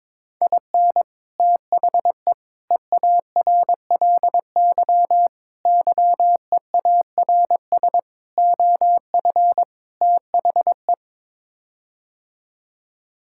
Morse code, heard as a keyed tone, about 22 wpm, 700 Hz. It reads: IN THE EARLY YEARS OF THE
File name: practice / 22wpm / Sets of 6 Words - 22wpm